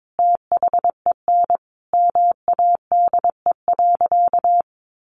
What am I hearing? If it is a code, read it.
Morse code, heard as a keyed tone, about 22 wpm, 700 Hz. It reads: THEN MADE.
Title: Sentences from Top 1500 Words 22wpm